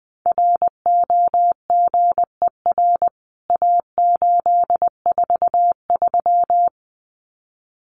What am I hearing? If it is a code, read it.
Morse code, heard as a keyed tone, about 20 wpm, 700 Hz. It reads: ROGER A843
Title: CWT Exchange 20wpm